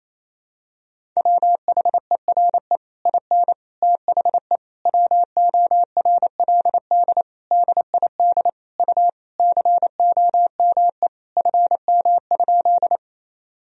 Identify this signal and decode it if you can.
Morse code, heard as a keyed tone, about 28 wpm, 700 Hz. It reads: WHERE IN THE WORLD DID U COME FM?
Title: Sentences from Top 200 Words and Abbreviations 28wpm